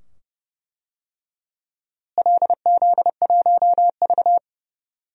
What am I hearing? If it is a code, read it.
Morse code, heard as a keyed tone, about 31 wpm, 700 Hz. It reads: LZ1V
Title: DXCC-100 Call Signs - 4 Characters - Repeated 2x - 30wpm